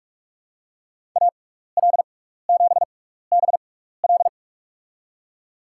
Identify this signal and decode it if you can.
Morse code, heard as a keyed tone, about 44 wpm, spaced as for 15 wpm, 700 Hz. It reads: AL7BL
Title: USA Call Signs - 5 Characters - 6x Character Spacing - 45wpm